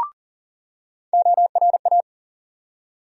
Morse code, heard as a keyed tone, about 40 wpm, 700 Hz. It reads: ORA
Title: USA Call Sign Postfixes - 3 Letters 40wpm